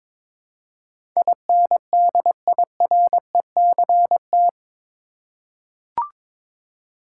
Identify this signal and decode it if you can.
Morse code, heard as a keyed tone, about 22 wpm, 700 Hz. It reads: INDIRECT
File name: practice / 22wpm / Eight Letter Words - 22wpm